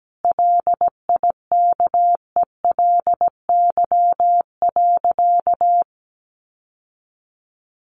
Morse code, heard as a keyed tone, about 17 wpm, 700 Hz. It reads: LIKELY.
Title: Sentences from Top 600 Words 17wpm